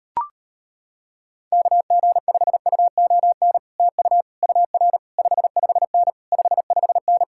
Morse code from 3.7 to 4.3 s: TU

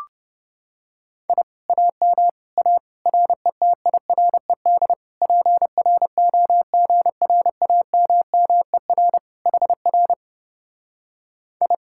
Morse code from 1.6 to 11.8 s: AM A RETIRED PROGRAMMER HR I

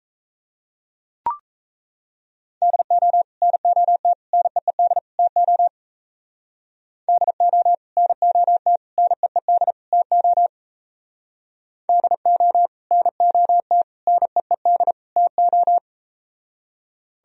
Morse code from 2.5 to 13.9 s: DO NOT NEED TO DO NOT NEED TO DO NOT

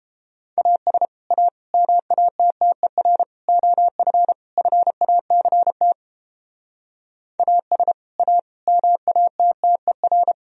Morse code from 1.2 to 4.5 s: A MATTER OF

